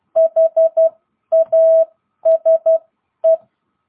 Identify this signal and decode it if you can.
Morse code, heard as a keyed tone, about 12 wpm, spaced as for 10 wpm, 660 Hz. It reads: HASE
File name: jh8cbh / 2022cw